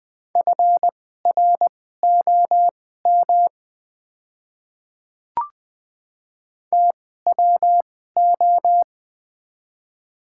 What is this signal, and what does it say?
Morse code, heard as a keyed tone, about 20 wpm, spaced as for 15 wpm, 700 Hz. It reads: FROM TWO
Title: 2x Character Spacing - Top 100 Words - 20wpm